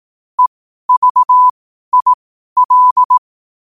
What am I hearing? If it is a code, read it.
Morse code, heard as a keyed tone, about 18 wpm, spaced as for 12 wpm, 1000 Hz. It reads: EVIL
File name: qotdfiles / 12wpm